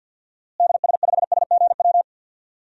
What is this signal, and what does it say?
Morse code, heard as a keyed tone, about 52 wpm, 700 Hz. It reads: DS5SGW
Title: DXCC-100 Call Signs - 6 Characters - Repeated 3x - 50wpm